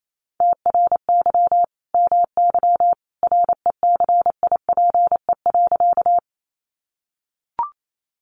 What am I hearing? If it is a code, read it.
Morse code, heard as a keyed tone, about 28 wpm, 700 Hz. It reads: TRY MY RECIPE.
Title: Sentences from Top 1200 Words 28wpm